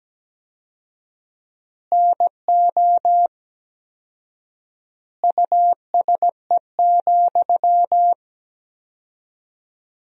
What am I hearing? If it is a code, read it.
Morse code, heard as a keyed tone, about 17 wpm, 700 Hz. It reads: NO USE,
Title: 4x Word-Spacing - Sentences from Top 500 Words - 17wpm